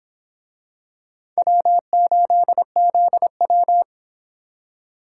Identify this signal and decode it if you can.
Morse code, heard as a keyed tone, about 25 wpm, 700 Hz. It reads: W8ZW